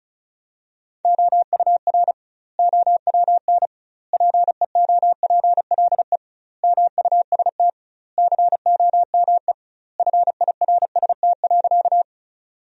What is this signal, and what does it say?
Morse code, heard as a keyed tone, about 34 wpm, 700 Hz. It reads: OUR OWN PEOPLE MUST COME FIRST.
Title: 2x Word-Spacing - Sentences from Top 300 Words - 35wpm